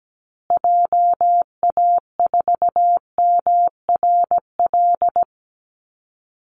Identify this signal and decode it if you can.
Morse code, heard as a keyed tone, about 17 wpm, 700 Hz. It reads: JA4MRL
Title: Call Signs Repeated 3x - 17wpm